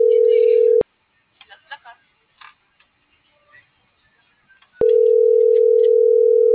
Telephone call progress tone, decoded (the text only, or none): ringback tone